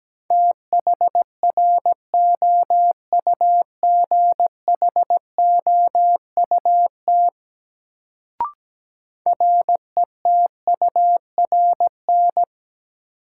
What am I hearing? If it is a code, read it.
Morse code, heard as a keyed tone, about 17 wpm, 700 Hz. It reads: THROUGHOUT RETURN
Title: Sentences from Top 800 Words - New Word Drill - 17wpm